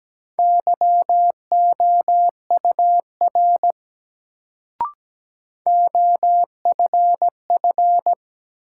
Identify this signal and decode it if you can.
Morse code, heard as a keyed tone, about 17 wpm, 700 Hz. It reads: YOUR OFF
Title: Sets of 2 Words - New Word Drill - 17wpm